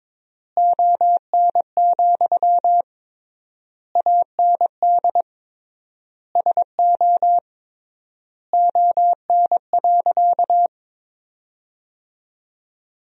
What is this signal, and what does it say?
Morse code, heard as a keyed tone, about 21 wpm, 700 Hz. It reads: ON, AND SO ON.
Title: 3x Word-Spacing - Sentences from Top 100 Words - 22wpm